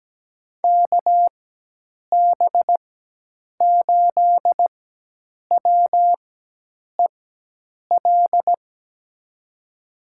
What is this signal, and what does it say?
Morse code, heard as a keyed tone, about 17 wpm, spaced as for 8 wpm, 700 Hz. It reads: KB8WEL
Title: USA Call Signs - 6 Characters - 4x Character Spacing - 17wpm